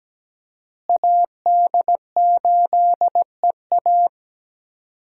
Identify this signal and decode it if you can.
Morse code, heard as a keyed tone, about 17 wpm, 700 Hz. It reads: AD8EA
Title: USA Call Signs - 5 Characters - Repeated 3x - 17wpm